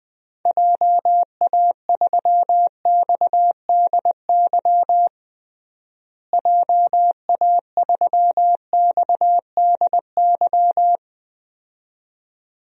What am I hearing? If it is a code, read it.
Morse code, heard as a keyed tone, about 20 wpm, 700 Hz. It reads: JA3XDY JA3XDY
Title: DXCC-100 Call Signs - 6 Characters - Repeated 2x - 20wpm